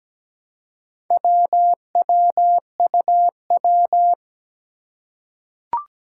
Morse code, heard as a keyed tone, about 17 wpm, 700 Hz. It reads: WWUW
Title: Four Challenging Letter-Number Combinations 17wpm